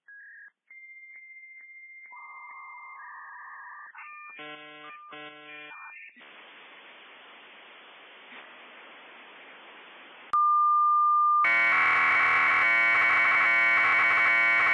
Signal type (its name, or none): dial-up modem handshake